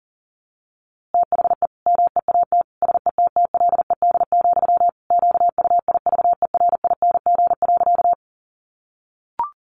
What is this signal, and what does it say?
Morse code, heard as a keyed tone, about 40 wpm, 700 Hz. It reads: THE MEAT SETTLED, QUIVERING.
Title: A Sound Of Thunder 40wpm